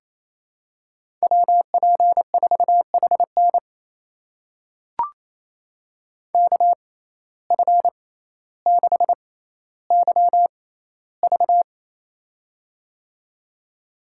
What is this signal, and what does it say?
Morse code, heard as a keyed tone, about 28 wpm, spaced as for 14 wpm, 700 Hz. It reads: WP4HN KF6YV
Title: USA Call Signs - 5 Characters - 6x Character Spacing - 28wpm